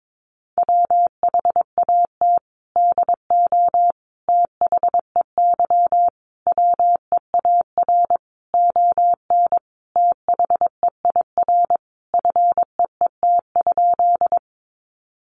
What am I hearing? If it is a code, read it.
Morse code, heard as a keyed tone, about 22 wpm, 700 Hz. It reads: WHAT DO THEY WEAR ON THEIR FEET?
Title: Sentences from Top 1500 Words 22wpm